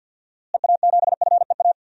Morse code, heard as a keyed tone, about 53 wpm, 700 Hz. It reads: EA7FEA
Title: DXCC-100 Call Signs - 6 Characters - Repeated 3x - 50wpm